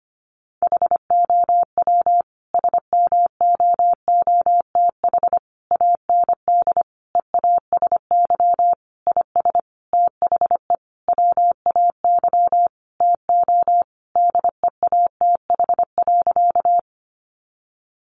Morse code, heard as a keyed tone, about 25 wpm, 700 Hz. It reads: HOW SMOOTH AND EASY IS THE WAY TO DEATH.
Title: Sentences from Top 1400 Words 25wpm